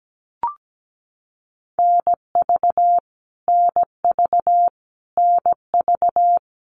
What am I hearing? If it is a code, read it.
Morse code, heard as a keyed tone, about 17 wpm, 700 Hz. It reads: NV NV NV
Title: US States Repeated 3x 17wpm